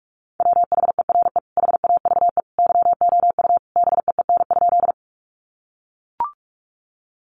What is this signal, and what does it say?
Morse code, heard as a keyed tone, about 44 wpm, 700 Hz. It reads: WHERE HAVE YOU BEEN?
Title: Sentences from Top 200 Words 45wpm